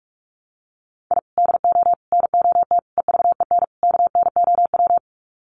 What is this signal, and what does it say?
Morse code, heard as a keyed tone, about 44 wpm, 700 Hz. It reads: I DO NOT EVEN KNOW
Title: Sets of 5 Words 45wpm